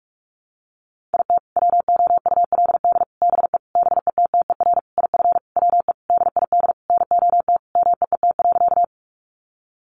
Morse code, heard as a keyed tone, about 44 wpm, 700 Hz. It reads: IT WOULD BE BETTER IF WE DID NOT MEET.
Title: Sentences from Top 1100 Words 45wpm